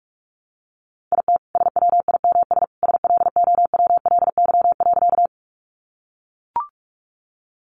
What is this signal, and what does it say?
Morse code, heard as a keyed tone, about 44 wpm, 700 Hz. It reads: IT SWIMS SLOWLY.
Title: Additional Easy Phrases 45wpm